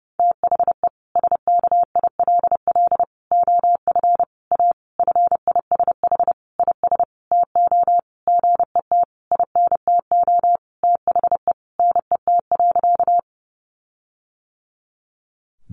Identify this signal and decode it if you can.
Morse code, heard as a keyed tone, about 30 wpm, 700 Hz. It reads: THE SKILL OF A FISH IS TO GET INTO THE NET.
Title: Sentences from Top 1400 Words 30wpm